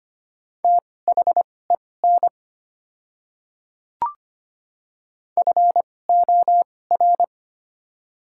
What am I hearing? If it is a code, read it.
Morse code, heard as a keyed tone, about 25 wpm, spaced as for 18 wpm, 690 Hz. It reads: THEN FOR
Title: 2x Character Spacing - Top 100 Words - 25wpm